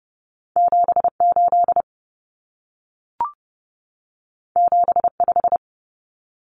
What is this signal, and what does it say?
Morse code, heard as a keyed tone, about 30 wpm, 700 Hz. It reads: 78 75
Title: Common Cut Numbers - 2-Digit 30wpm